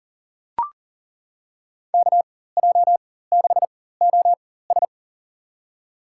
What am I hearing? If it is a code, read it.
Morse code, heard as a keyed tone, about 40 wpm, spaced as for 19 wpm, 700 Hz. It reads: KJ6OS
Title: USA Call Signs - 5 Characters - 4x Character Spacing - 40wpm